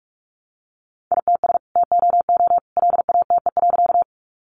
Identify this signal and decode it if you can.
Morse code, heard as a keyed tone, about 44 wpm, 700 Hz. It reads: ITS TOO LATE.
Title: Sentences from Top 600 Words 45wpm